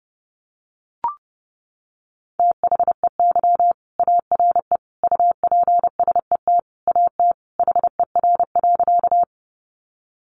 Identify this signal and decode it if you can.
Morse code, heard as a keyed tone, about 30 wpm, 700 Hz. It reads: THEY ARE UPSET AT HER.
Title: Sentences from Top 800 Words 30wpm